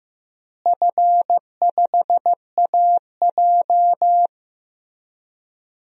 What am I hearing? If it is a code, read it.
Morse code, heard as a keyed tone, about 15 wpm, 700 Hz. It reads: F5AJ
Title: DXCC-100 Call Signs - 4 Characters - Repeated 3x - 15wpm